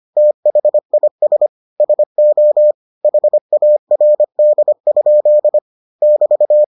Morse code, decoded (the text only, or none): THIS SO HARD? <BT>